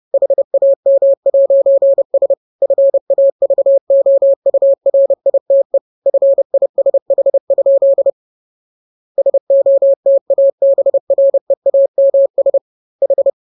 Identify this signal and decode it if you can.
Morse code, heard as a keyed tone, about 30 wpm, 550 Hz. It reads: HAM'S FAVOURITE FISH? SOTABREAMS H
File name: jokes / joke 6 30 0000